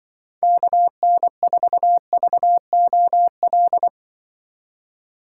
Morse code, heard as a keyed tone, about 24 wpm, 700 Hz. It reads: KN4VOL
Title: Speed Racing - USA 2x3 Call Signs 20wpm